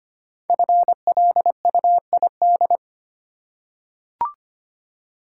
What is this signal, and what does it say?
Morse code, heard as a keyed tone, about 25 wpm, 700 Hz. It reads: FLUID
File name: 5 Letter Words - First 3 Letters Given - 25wpm